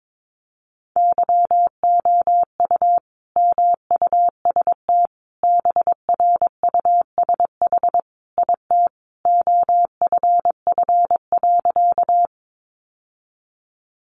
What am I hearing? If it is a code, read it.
Morse code, heard as a keyed tone, about 22 wpm, 700 Hz. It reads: YOU MUST BRUSH IT OFF.
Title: Sentences from Top 1100 Words 22wpm